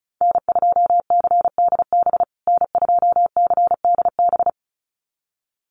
Morse code, heard as a keyed tone, about 34 wpm, 700 Hz. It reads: N2CDB N2CDB
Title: USA Call Signs Repeated 2x 35wpm